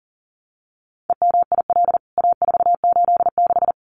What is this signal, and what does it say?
Morse code, heard as a keyed tone, about 37 wpm, 700 Hz. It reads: EMIL A486
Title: CWT Exchange 40wpm